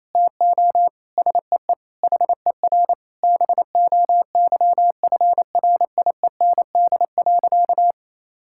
Morse code, heard as a keyed tone, about 28 wpm, 690 Hz. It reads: TO SEE HER BOYFRIEND.
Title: Sentences from Top 1000 Words - Review Part 6 - 28wpm